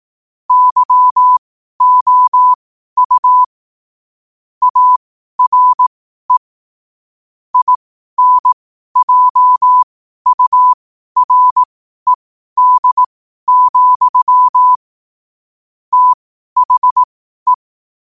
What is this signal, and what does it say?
Morse code, heard as a keyed tone, about 18 wpm, spaced as for 12 wpm, 1000 Hz. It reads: YOU ARE INJURED, THE
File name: qotdfiles / 12wpm